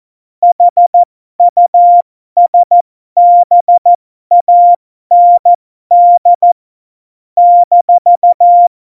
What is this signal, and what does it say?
Morse code, heard as a keyed tone, about 14 wpm, spaced as for 12 wpm, 700 Hz. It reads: HUSBAND -
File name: Morsekurs / Witze